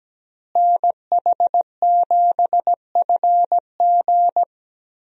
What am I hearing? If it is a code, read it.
Morse code, heard as a keyed tone, about 17 wpm, 700 Hz. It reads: NH7FG
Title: USA Call Signs - 5 Characters - Repeated 2x - 17wpm